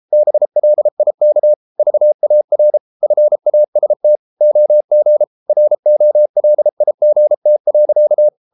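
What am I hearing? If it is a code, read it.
Morse code, heard as a keyed tone, about 33 wpm, 600 Hz. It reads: BLIK VAR FAST OG ROLIGT.
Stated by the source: Morse code at 33 wpm, QRQ.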